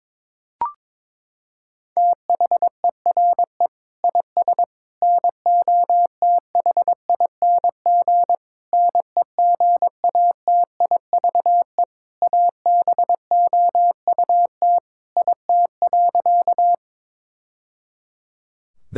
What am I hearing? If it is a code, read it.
Morse code, heard as a keyed tone, about 22 wpm, 700 Hz. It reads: THERE IS NOTHING NEGATIVE ABOUT IT.
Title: Sentences from Top 1000 Words - Review Part 6 - 22wpm